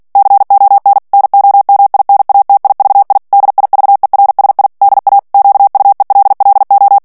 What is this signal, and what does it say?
Morse code, heard as a keyed tone, about 48 wpm, 800 Hz. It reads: CON NOMINATIVI DIVERSI DA QUELLO
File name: ITALIEN texte clair 2 vitesse 45 mots minute